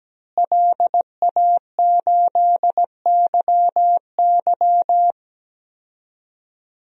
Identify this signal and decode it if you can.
Morse code, heard as a keyed tone, about 17 wpm, 700 Hz. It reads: LA8YY
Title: DXCC-100 Call Signs - 5 Characters - Repeated 3x - 17wpm